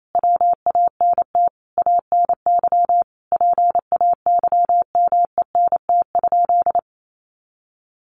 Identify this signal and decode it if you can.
Morse code, heard as a keyed tone, about 28 wpm, 700 Hz. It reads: WANT ANY PAYMENT?
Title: Sentences from Top 1200 Words 28wpm